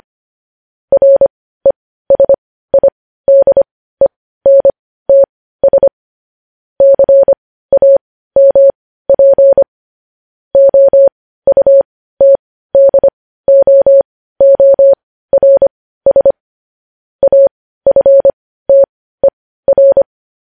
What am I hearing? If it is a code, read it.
Morse code, heard as a keyed tone, about 25 wpm, spaced as for 15 wpm, 550 Hz. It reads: RESIDENTS CAMP OUTDOORS AFTER
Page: news-headlines-in-morse-code-at-15-wpm